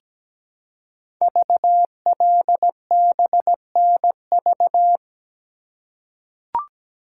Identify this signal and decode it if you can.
Morse code, heard as a keyed tone, about 17 wpm, 700 Hz. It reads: VLBNV
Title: Code Groups - 5 Letter:Numbers - 17wpm